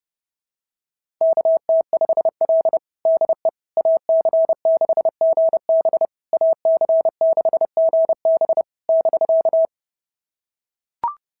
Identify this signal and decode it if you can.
Morse code, heard as a keyed tone, about 30 wpm, 650 Hz. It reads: KT5L DE AC6GB AC6GB <BK>
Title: SOTA QSO Practice 30wpm